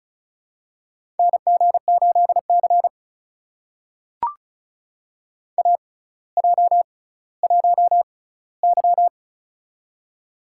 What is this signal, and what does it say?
Morse code, heard as a keyed tone, about 36 wpm, spaced as for 18 wpm, 700 Hz. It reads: NG8C AJ1Y
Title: USA Call Signs - 4 Characters - 6x Character Spacing - 35wpm